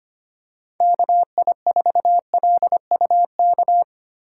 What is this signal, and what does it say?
Morse code, heard as a keyed tone, about 25 wpm, 700 Hz. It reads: KI4LUK
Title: USA Call Signs - 2x Character Spacing - 25wpm